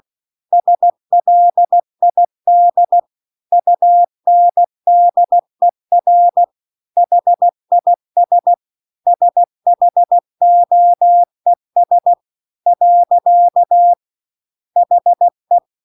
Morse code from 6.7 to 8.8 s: HIS